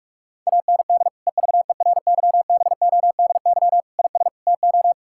Morse code, heard as a keyed tone, about 41 wpm, 700 Hz. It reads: AND EVERYBODY IS TO
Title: Sentences from Top 1600 Words 45wpm